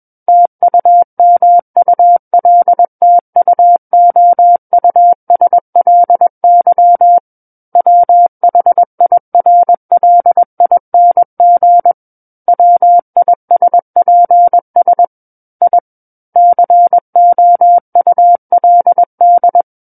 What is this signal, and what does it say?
Morse code, heard as a keyed tone, about 21 wpm, 700 Hz. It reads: TUMULTUOUSLY WHIRLING WISPS I COULD
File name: Book One / 12-Chapter 12 - 21 WPM